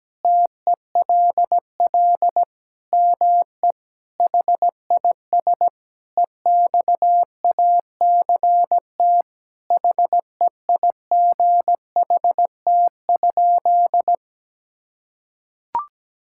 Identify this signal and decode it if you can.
Morse code, heard as a keyed tone, about 17 wpm, 700 Hz. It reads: TELL ME HIS EXACT HEIGHT?
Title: Sentences from Top 1600 Words 17wpm